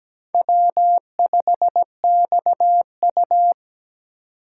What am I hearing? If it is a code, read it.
Morse code, heard as a keyed tone, about 20 wpm, 700 Hz. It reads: W5XU
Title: Speed Racing - USA 1x2 Call Signs 17wpm